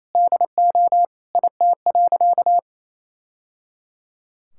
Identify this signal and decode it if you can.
Morse code, heard as a keyed tone, about 27 wpm, 700 Hz. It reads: DO IT.
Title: Sentences from Top 1000 Words - Review Part 5 - 28wpm